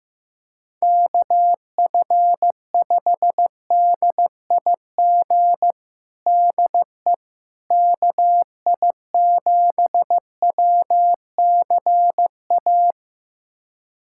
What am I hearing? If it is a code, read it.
Morse code, heard as a keyed tone, about 15 wpm, 700 Hz. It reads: KF5DIG DE KI7WCA
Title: USA Callsign DE Callsign 15wpm